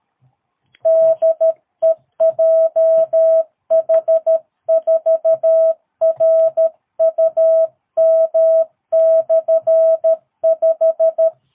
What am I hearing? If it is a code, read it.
Morse code, heard as a keyed tone, about 13 wpm, 650 Hz. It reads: DE JH4RUM/5
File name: jh8cbh / 2022cw